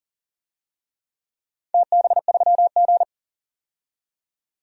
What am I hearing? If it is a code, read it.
Morse code, heard as a keyed tone, about 40 wpm, 700 Hz. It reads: TB3G
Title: DXCC-100 Call Signs - 4 Characters - 40wpm